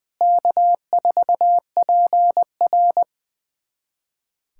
Morse code, heard as a keyed tone, about 20 wpm, 700 Hz. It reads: K4PR
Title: USA Callsign DE Callsign - 4 Characters - 20wpm